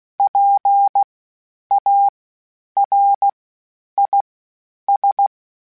Morse code, heard as a keyed tone, about 16 wpm, spaced as for 9 wpm, 800 Hz. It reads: PARIS
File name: PARIS Calibration / PARIS mp3s / PARIS@10-@15